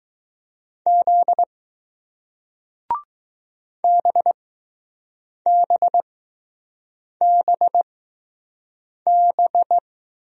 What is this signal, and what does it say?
Morse code, heard as a keyed tone, about 20 wpm, spaced as for 6 wpm, 700 Hz. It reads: Z BBBB